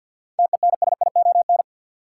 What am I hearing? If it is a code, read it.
Morse code, heard as a keyed tone, about 52 wpm, 700 Hz. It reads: TENSION
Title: Sentences from Top 1600 Words - New Word Drill - 50wpm